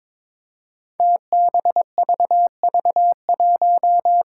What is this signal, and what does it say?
Morse code, heard as a keyed tone, about 22 wpm, 700 Hz. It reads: TBVV1
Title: Five Challenging Letter-Number Combinations 22wpm